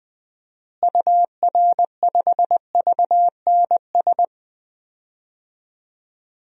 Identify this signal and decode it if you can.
Morse code, heard as a keyed tone, about 21 wpm, 700 Hz. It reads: UR5VNS
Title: DXCC-100 Call Signs - 6 Characters - Repeated 2x - 20wpm